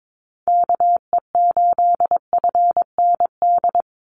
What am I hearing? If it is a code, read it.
Morse code, heard as a keyed tone, about 22 wpm, 700 Hz. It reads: KE8FND
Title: USA Call Signs 22wpm